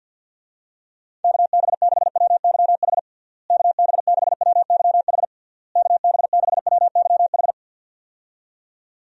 Morse code, heard as a keyed tone, about 52 wpm, 700 Hz. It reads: KB6WYH KB6WYH KB6WYH
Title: USA Call Signs - 6 Characters - Repeated 3x - 50wpm